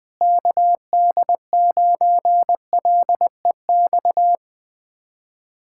Morse code, heard as a keyed tone, about 20 wpm, 700 Hz. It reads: KD9LEX
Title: USA Callsign DE Callsign - 6 Characters - 20wpm